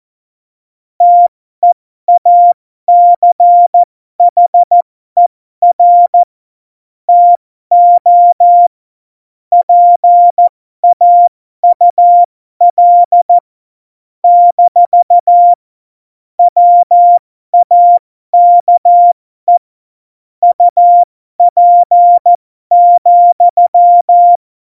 Morse code, heard as a keyed tone, about 14 wpm, spaced as for 12 wpm, 700 Hz. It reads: TEACHER TO PAUL - WAKE UP,
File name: Morsekurs / Witze